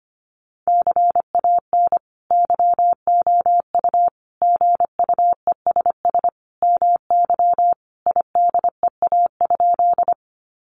Morse code, heard as a keyed tone, about 25 wpm, 700 Hz. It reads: CAN YOU GUESS MY IDEA?
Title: Sentences from Top 1500 Words 25wpm